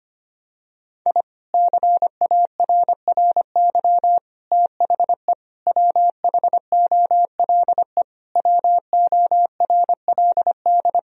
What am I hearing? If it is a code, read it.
Morse code, heard as a keyed tone, about 25 wpm, 700 Hz. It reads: I CARRY THE WHOLE WORLD
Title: Sentences from Top 1000 Words - Review Part 1 - 25wpm